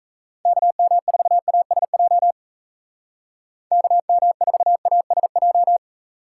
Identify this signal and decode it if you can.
Morse code, heard as a keyed tone, about 34 wpm, 700 Hz. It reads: KM4ASJ KM4ASJ
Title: Speed Racing - USA 2x3 Call Signs 28wpm